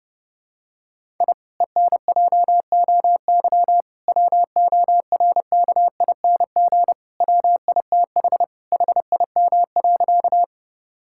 Morse code, heard as a keyed tone, about 30 wpm, 700 Hz. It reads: I ENJOY WORKING WITH HIM.
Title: Sentences from Top 1000 Words - Review Part 5 - 30wpm